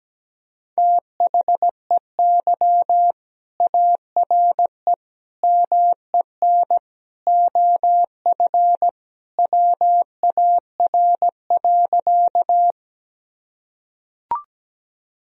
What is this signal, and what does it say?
Morse code, heard as a keyed tone, about 17 wpm, 700 Hz. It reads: THEY ARE MEN OF WAR.
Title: Sentences from Top 1000 Words - Review Part 2 - 17wpm